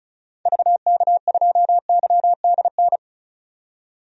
Morse code, heard as a keyed tone, about 36 wpm, 700 Hz. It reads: VK2YDN
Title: DXCC-100 Call Signs - 6 Characters - 35wpm